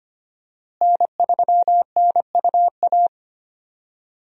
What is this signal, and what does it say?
Morse code, heard as a keyed tone, about 25 wpm, 700 Hz. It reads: N3NUA
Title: USA Call Signs - 5 Characters - Repeated 3x - 25wpm